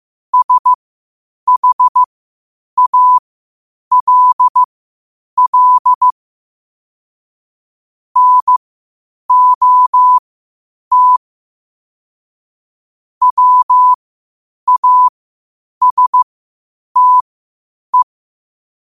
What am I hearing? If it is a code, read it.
Morse code, heard as a keyed tone, about 15 wpm, spaced as for 8 wpm, 1000 Hz. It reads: SHALL NOT WASTE
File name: qotdfiles / 8wpm